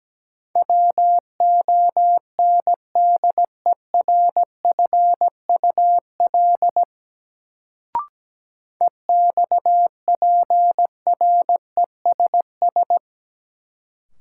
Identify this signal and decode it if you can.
Morse code, heard as a keyed tone, about 17 wpm, 700 Hz. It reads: WONDERFUL EXPRESS
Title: Sentences from Top 1100 Words - New Word Drill - 17wpm